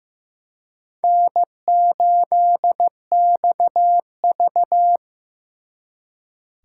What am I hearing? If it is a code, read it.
Morse code, heard as a keyed tone, about 18 wpm, 700 Hz. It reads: N8XV
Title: Speed Racing - USA 1x2 Call Signs 15wpm